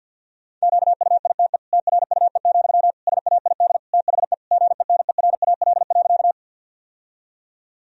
Morse code, heard as a keyed tone, about 51 wpm, 700 Hz. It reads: QUITE TRUE, SAID THE GENERAL.
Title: Sentences from Top 1000 Words - Review Part 3 - 50wpm